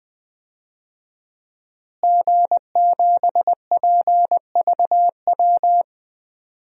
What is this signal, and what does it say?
Morse code, heard as a keyed tone, about 20 wpm, 700 Hz. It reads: G7PVW
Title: DXCC-100 Call Signs - 5 Characters - Repeated 3x - 20wpm